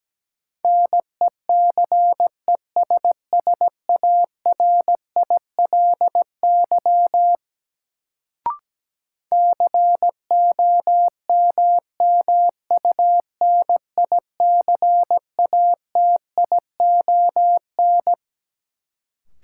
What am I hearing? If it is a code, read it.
Morse code, heard as a keyed tone, about 17 wpm, 700 Hz. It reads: NECESSARILY COMMUNICATION
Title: Sentences from Top 1400 Words - New Word Drill Alt - 17wpm